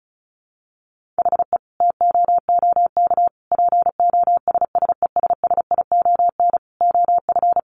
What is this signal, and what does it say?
Morse code, heard as a keyed tone, about 34 wpm, 700 Hz. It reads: HE TOOK POSSESSION OF
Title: Sentences from Top 1500 Words 35wpm